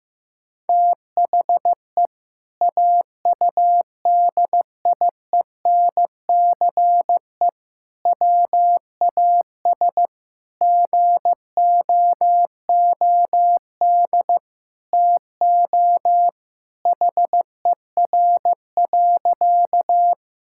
Morse code, heard as a keyed tone, about 14 wpm, 700 Hz. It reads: THE AUDIENCE WAS GOOD TO HER.
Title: Sentences from Top 1000 Words - Review Part 5 - 15wpm